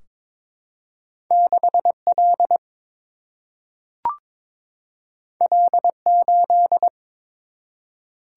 Morse code, heard as a keyed tone, about 22 wpm, 700 Hz. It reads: 6 L L 8
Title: Single-Letter Single-Number 22wpm